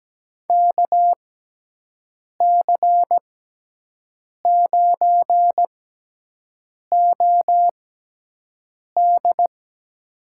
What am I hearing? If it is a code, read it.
Morse code, heard as a keyed tone, about 17 wpm, spaced as for 6 wpm, 700 Hz. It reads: KC9OD
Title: USA Call Signs - 5 Characters - 6x Character Spacing - 17wpm